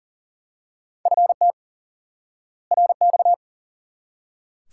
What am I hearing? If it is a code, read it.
Morse code, heard as a keyed tone, about 40 wpm, 700 Hz. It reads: FT RX
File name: IQE Course 3 - Lesson 89 - YAGI - Two Elements - 40wpm